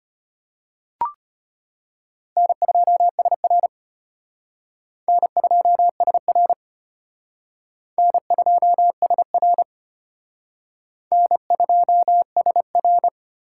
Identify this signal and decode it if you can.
Morse code, heard as a keyed tone, about 31 wpm, 700 Hz. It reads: N2SR N2SR N2SR N2SR
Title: Speed Racing - USA 1x2 Call Signs 25wpm